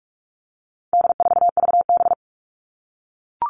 Morse code, heard as a keyed tone, about 44 wpm, 700 Hz. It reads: D4VB
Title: All Cut Numbers - 4-Digit 45wpm